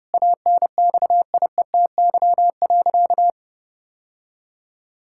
Morse code, heard as a keyed tone, about 30 wpm, 700 Hz. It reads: ANXIETY.
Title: Sentences from Top 1600 Words 30wpm